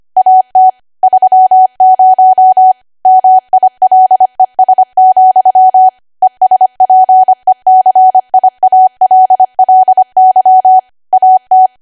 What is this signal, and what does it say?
Morse code, heard as a keyed tone, about 25 wpm, 750 Hz. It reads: AT 30 MILES, ESPECIALLY AT
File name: Archive / 25 WPM / 25WPM